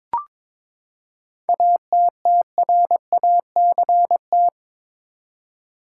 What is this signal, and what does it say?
Morse code, heard as a keyed tone, about 22 wpm, 700 Hz. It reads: ATTRACT
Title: Seven Letter Words - 22wpm